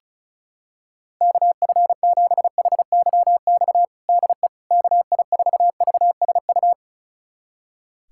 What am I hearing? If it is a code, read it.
Morse code, heard as a keyed tone, about 34 wpm, 700 Hz. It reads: KF7HYX DE KI4VSU
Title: USA Callsign DE Callsign - 6 Characters - 35wpm